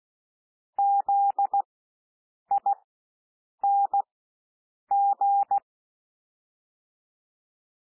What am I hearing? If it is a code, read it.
Morse code, heard as a keyed tone, about 16 wpm, spaced as for 7 wpm, 800 Hz. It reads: ZING
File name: cwmp3 / W1 SOUND WORDS 16 wpm